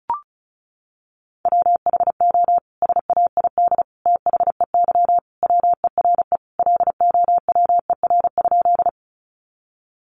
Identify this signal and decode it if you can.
Morse code, heard as a keyed tone, about 34 wpm, 700 Hz. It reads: WHO SAID THEY WERE LOWER?
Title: Sentences from Top 700 Words 35wpm